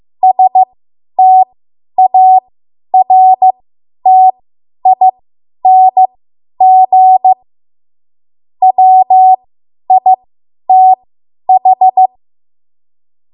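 Morse code, heard as a keyed tone, about 15 wpm, spaced as for 10 wpm, 750 Hz. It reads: STARTING WITH
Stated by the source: Morse code practice run at 10 WPM from a code practice archive.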